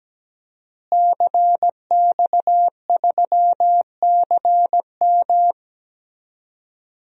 Morse code, heard as a keyed tone, about 17 wpm, 700 Hz. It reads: CX3CM
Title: DXCC-100 Call Signs - 5 Characters - Repeated 3x - 17wpm